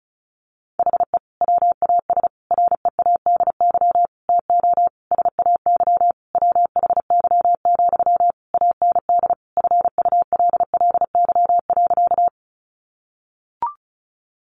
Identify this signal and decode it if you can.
Morse code, heard as a keyed tone, about 34 wpm, 700 Hz. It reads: HE WAS READY TO SAY WHY, AND FULLY.
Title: Sentences from Top 1300 Words 35wpm